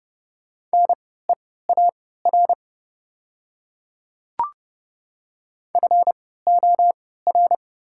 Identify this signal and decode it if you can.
Morse code, heard as a keyed tone, about 30 wpm, spaced as for 17 wpm, 700 Hz. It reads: NEAR FOR
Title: 3x Character Spacing - Top 200 Words - 30wpm